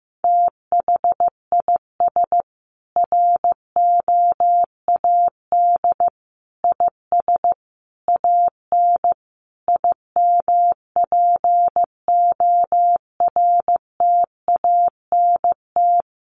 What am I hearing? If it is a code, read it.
Morse code, heard as a keyed tone, about 15 wpm, 700 Hz. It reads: THIS ROAD IS AN IMPORTANT
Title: Sentences from Top 1400 Words 15wpm